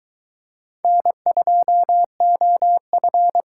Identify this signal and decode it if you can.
Morse code, heard as a keyed tone, about 23 wpm, 700 Hz. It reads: N2OF